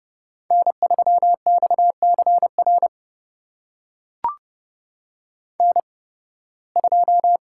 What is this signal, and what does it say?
Morse code, heard as a keyed tone, about 30 wpm, spaced as for 19 wpm, 700 Hz. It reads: N3XCR N2
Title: USA Call Signs - 5 Characters - 8x Character Spacing - 30wpm